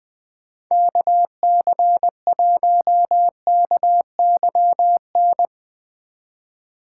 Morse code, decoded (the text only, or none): KC1KYN